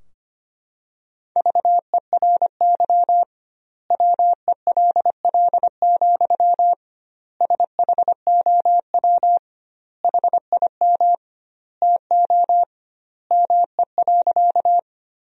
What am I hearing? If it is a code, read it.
Morse code, heard as a keyed tone, about 25 wpm, 700 Hz. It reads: VERY WELL, SHOW HIM TO ME.
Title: 2x Word-Spacing - Sentences from Top 200 Words - 25wpm